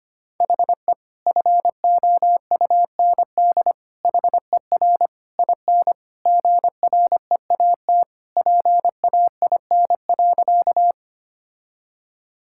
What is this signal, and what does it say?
Morse code, heard as a keyed tone, about 25 wpm, 700 Hz. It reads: HE FOUND HER IN GREAT PAIN.
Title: Sentences from Top 1600 Words 25wpm